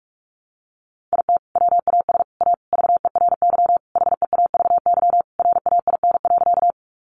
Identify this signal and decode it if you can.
Morse code, heard as a keyed tone, about 44 wpm, 700 Hz. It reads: IT WAS A VERY HEAVY RAIN.
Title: Sentences from Top 1100 Words 45wpm